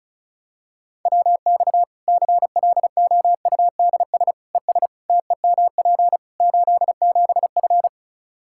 Morse code, heard as a keyed tone, about 36 wpm, 700 Hz. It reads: WX CLOUDS ES TEMP 87F
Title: Practice QSO - Exchange 2 - Info - 35wpm